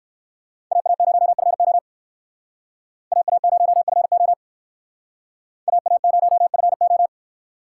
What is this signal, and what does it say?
Morse code, heard as a keyed tone, about 57 wpm, 700 Hz. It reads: AA0FO AA0FO AA0FO